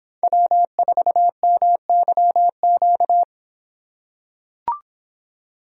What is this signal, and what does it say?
Morse code, heard as a keyed tone, about 24 wpm, 700 Hz. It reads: W4MYQ